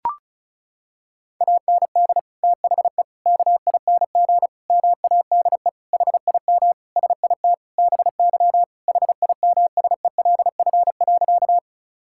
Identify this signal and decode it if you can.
Morse code, heard as a keyed tone, about 34 wpm, 700 Hz. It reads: AND THE KING MADE HIM SIT BY HIMSELF.
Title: Sentences from Top 1600 Words 35wpm